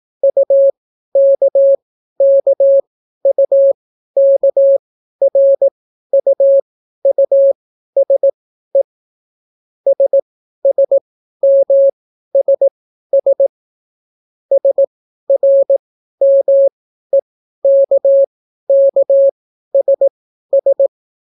Morse code, heard as a keyed tone, about 18 wpm, spaced as for 12 wpm, 550 Hz. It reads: UKKUKRUUSE SSMSS SRMEKKSS